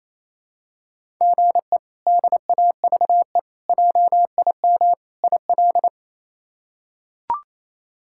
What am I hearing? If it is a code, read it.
Morse code, heard as a keyed tone, about 28 wpm, 700 Hz. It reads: GE DAVE JIM IL
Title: K1USN SST QSO Practice - 28wpm